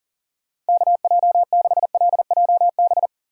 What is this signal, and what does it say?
Morse code, heard as a keyed tone, about 40 wpm, 700 Hz. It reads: KJ6LJB
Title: USA Call Signs - 2x Character Spacing - 40wpm